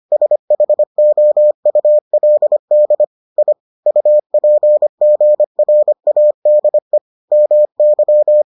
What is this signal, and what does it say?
Morse code, heard as a keyed tone, about 25 wpm, 600 Hz. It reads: SHOULD I UPGRADE MY